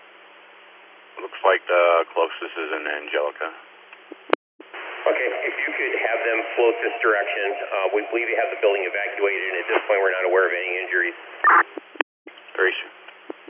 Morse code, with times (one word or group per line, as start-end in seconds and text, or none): none